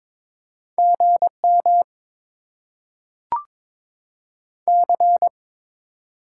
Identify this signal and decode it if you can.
Morse code, heard as a keyed tone, about 22 wpm, 700 Hz. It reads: GM C
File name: Top 20 QSO Elements - 6x Character Spacing - 22wpm